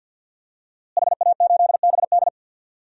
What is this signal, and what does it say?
Morse code, heard as a keyed tone, about 47 wpm, 700 Hz. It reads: HA8BD
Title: DXCC-100 Call Signs - 5 Characters - Repeated 2x - 50wpm